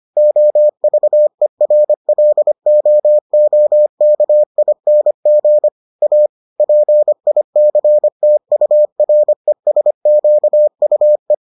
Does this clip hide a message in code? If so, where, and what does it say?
Yes: Morse code from 0.2 to 11.3 s, heard as a keyed tone, about 25 wpm, 600 Hz. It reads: OVERLOOKING A PICTURESQUE